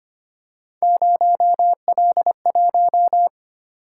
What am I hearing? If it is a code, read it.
Morse code, heard as a keyed tone, about 26 wpm, 700 Hz. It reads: 0L1
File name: Three Letter-Number Combinations - 25wpm